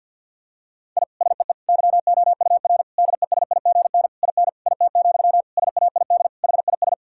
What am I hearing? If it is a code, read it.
Morse code, heard as a keyed tone, about 50 wpm, 700 Hz. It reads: I SEE YOUR DESIGN IN IT, SAID HIS
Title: Sentences from Top 1000 Words - Review Part 5 - 50wpm